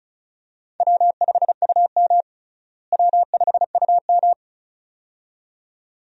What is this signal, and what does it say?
Morse code, heard as a keyed tone, about 36 wpm, 700 Hz. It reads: W5UM W5UM
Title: USA Call Signs - 4 Characters - Repeated 2x - 35wpm